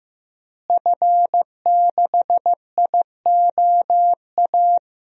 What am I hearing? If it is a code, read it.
Morse code, heard as a keyed tone, about 15 wpm, 700 Hz. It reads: F6IOA
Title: DXCC-100 Call Signs - 5 Characters - Repeated 3x - 15wpm